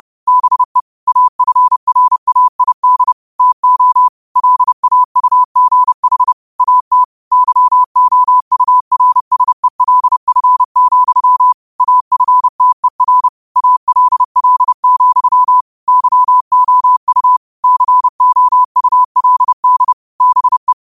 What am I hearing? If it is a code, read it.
Morse code, heard as a keyed tone, about 30 wpm, 1000 Hz. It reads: BE AFRAID TO LAUGH AT YOURSELF, AFTER ALL, YOU COULD BE